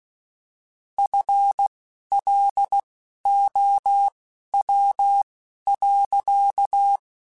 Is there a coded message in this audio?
Yes: Morse code from 1.0 to 7.0 s, heard as a keyed tone, about 16 wpm, spaced as for 11 wpm, 800 Hz. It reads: FLOW.